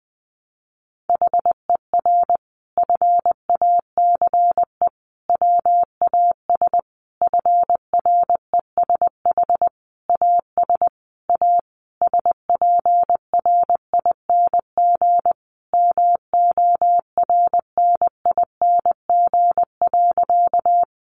Morse code, heard as a keyed tone, about 20 wpm, 700 Hz. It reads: HER FACE WAS FRESH AS A SPRING MORNING.
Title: Sentences from Top 1200 Words 20wpm